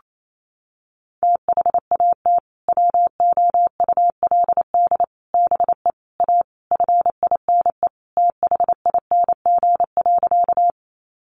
Morse code, heard as a keyed tone, about 28 wpm, 700 Hz. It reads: THAT WOULD BE A FINE THING.
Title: Sentences from Top 500 Words 28wpm